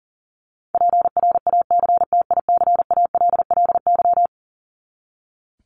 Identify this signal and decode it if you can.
Morse code, heard as a keyed tone, about 40 wpm, 700 Hz. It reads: PRACTICALLY
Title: LY Suffix - Top 100 Words with Base Words 40wpm